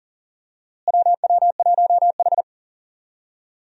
Morse code, heard as a keyed tone, about 39 wpm, 700 Hz. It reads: WW1H
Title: USA Call Signs - 4 Characters - Repeated 3x - 40wpm